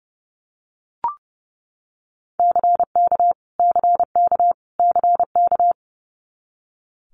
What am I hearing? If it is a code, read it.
Morse code, heard as a keyed tone, about 30 wpm, 700 Hz. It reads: CK CK CK
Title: Multi-letter Phonemes Repeated 3x 30wpm